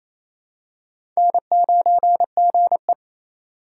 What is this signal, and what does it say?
Morse code, heard as a keyed tone, about 28 wpm, 700 Hz. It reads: N9GE
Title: USA Call Signs - 4 Characters - 3x Character Spacing - 28wpm